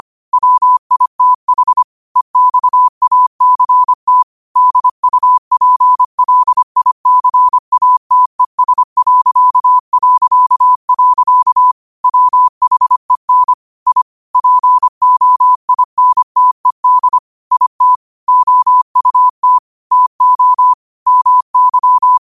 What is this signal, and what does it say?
Morse code, heard as a keyed tone, about 25 wpm, 1000 Hz. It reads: WITH EXACT DUPLICATES... WHEN I POINTED IT OUT TO MY